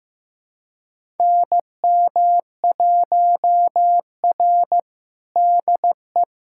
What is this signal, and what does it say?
Morse code, heard as a keyed tone, about 15 wpm, 700 Hz. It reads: NM1R DE
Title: USA Callsign DE Callsign - 4 Characters - 15wpm